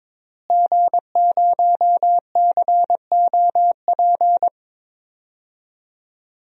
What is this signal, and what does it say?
Morse code, heard as a keyed tone, about 22 wpm, 700 Hz. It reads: G0COP
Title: DXCC-100 Call Signs - 5 Characters - Repeated 3x - 22wpm